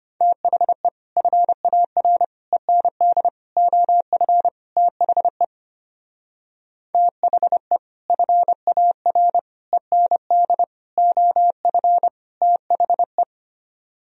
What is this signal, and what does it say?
Morse code, heard as a keyed tone, about 29 wpm, 700 Hz. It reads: THE FAR END OF THE THE FAR END OF THE
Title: Speed-Racing - Sets of 5 Words 25wpm